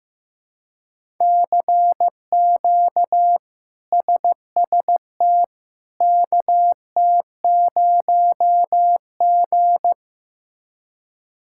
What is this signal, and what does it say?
Morse code, heard as a keyed tone, about 15 wpm, 700 Hz. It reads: CQ SST KT0G
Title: K1USN SST QSO Practice - 15wpm